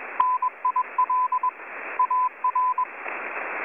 Morse code, heard as a keyed tone, about 21 wpm, 1010 Hz, with radio noise. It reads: NIL AR
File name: M42b-10483khz-K4MT-1500-to-1507